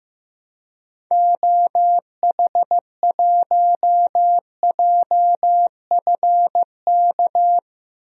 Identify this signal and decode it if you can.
Morse code, heard as a keyed tone, about 15 wpm, 700 Hz. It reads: OH1JFK
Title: DXCC-100 Call Signs - 6 Characters - 15wpm